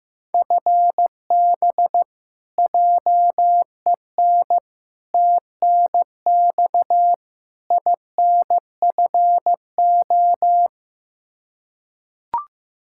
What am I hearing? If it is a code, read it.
Morse code, heard as a keyed tone, about 16 wpm, 700 Hz. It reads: FB JEN TNX INFO
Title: Practice QSO - Exchange 3 - More Info - 15wpm